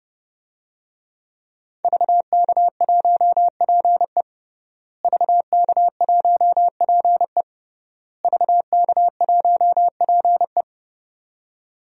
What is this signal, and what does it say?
Morse code, heard as a keyed tone, about 30 wpm, 700 Hz. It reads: VK1PE VK1PE VK1PE
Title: DXCC-100 Call Signs - 5 Characters - Repeated 3x - 30wpm